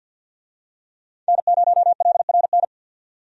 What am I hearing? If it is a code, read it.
Morse code, heard as a keyed tone, about 52 wpm, 700 Hz. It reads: N0LRN
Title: USA Call Signs - 5 Characters - 50wpm